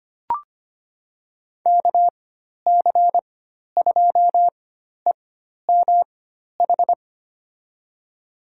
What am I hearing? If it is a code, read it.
Morse code, heard as a keyed tone, about 25 wpm, spaced as for 12 wpm, 700 Hz. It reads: KC2EMH
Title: USA Call Signs - 6 Characters - 4x Character Spacing - 25wpm